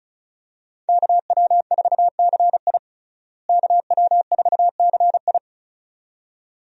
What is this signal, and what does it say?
Morse code, heard as a keyed tone, about 36 wpm, 700 Hz. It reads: KW4CI KW4CI
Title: USA Call Signs - 5 Characters - Repeated 2x - 35wpm